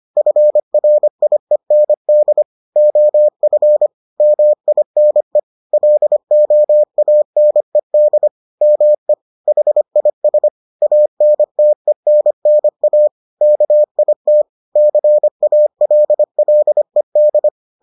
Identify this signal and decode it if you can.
Morse code, heard as a keyed tone, about 25 wpm, 600 Hz. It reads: FRIEND OF MINE LOANED ME HIS ANTENNA KIT CALLED